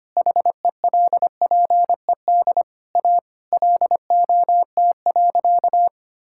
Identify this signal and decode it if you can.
Morse code, heard as a keyed tone, about 25 wpm, 700 Hz. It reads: HELPED A LOT.
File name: practice / 25wpm / Sentences with 3-Word Phrases - 25wpm